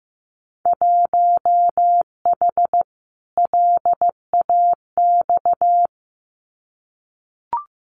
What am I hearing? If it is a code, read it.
Morse code, heard as a keyed tone, about 15 wpm, 700 Hz. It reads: 1H LAX
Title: Winter Field Day Exchanges 15wpm